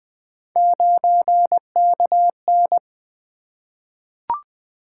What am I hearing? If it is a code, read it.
Morse code, heard as a keyed tone, about 20 wpm, 700 Hz. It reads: 9KN
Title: Three Letter-Number Combinations - 20wpm